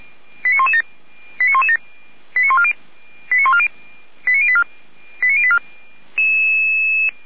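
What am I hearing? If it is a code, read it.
Five-tone selective call (ZVEI), heard as quick runs of five tones, tones of about 70 ms. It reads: 89117 89117 89140 89084 89084
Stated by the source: ZVEI five-tone call 89099 received by SDR, channel 455, recorded by feuerwehr-orlamuende fire brigade server.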